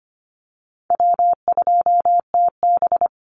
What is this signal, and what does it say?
Morse code, heard as a keyed tone, about 25 wpm, 700 Hz. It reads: W2TB
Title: Call Signs - 25wpm